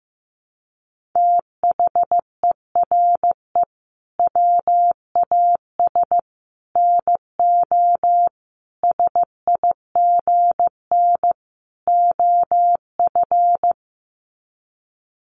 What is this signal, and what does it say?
Morse code, heard as a keyed tone, about 14 wpm, 700 Hz. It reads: THERE WAS NO SIGN OF
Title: Sets of 5 Words Encore2 15wpm